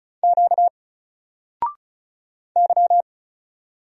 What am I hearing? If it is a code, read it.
Morse code, heard as a keyed tone, about 36 wpm, 700 Hz. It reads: Q Y
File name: Single-Letter-Number-35wpm